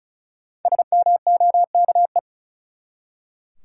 Morse code, heard as a keyed tone, about 34 wpm, 700 Hz. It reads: SMOKE
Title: Five Letter Words - 35wpm